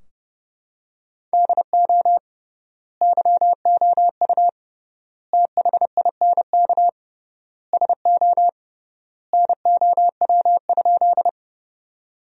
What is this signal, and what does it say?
Morse code, heard as a keyed tone, about 30 wpm, 700 Hz. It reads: DO YOU THINK SO NOW?
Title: 3x Word-Spacing - Sentences from Top 200 Words - 30wpm